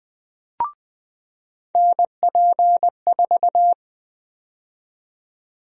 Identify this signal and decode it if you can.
Morse code, heard as a keyed tone, about 20 wpm, 700 Hz. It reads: NP4
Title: USA Call Sign Prefixes - 20wpm